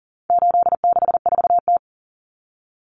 Morse code, heard as a keyed tone, about 40 wpm, 700 Hz. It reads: 864T
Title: Common Cut Numbers - 4-Digit 40wpm